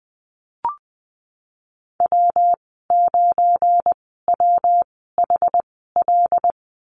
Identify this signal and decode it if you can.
Morse code, heard as a keyed tone, about 19 wpm, spaced as for 14 wpm, 700 Hz. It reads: W9WHL
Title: 186 Lesson 30g - Letter b - Call Signs 20wpm